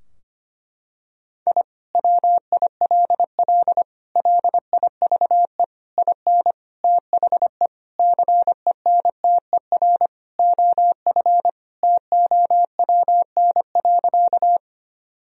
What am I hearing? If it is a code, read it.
Morse code, heard as a keyed tone, about 25 wpm, 700 Hz. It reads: I WILL LIVE IN THE CENTER OF TOWN.
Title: Sentences from Top 1000 Words - Review Part 2 - 25wpm